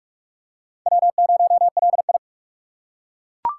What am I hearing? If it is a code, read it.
Morse code, heard as a keyed tone, about 41 wpm, 700 Hz. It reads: W0LI
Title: USA Call Signs - 3x Character Spacing - 45wpm